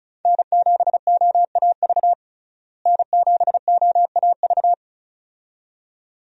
Morse code, heard as a keyed tone, about 36 wpm, 700 Hz. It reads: N7OAV N7OAV
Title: USA Call Signs - 5 Characters - Repeated 2x - 35wpm